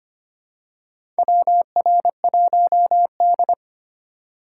Morse code, heard as a keyed tone, about 25 wpm, 700 Hz. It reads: WR1D
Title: USA Call Signs - 2x Character Spacing - 25wpm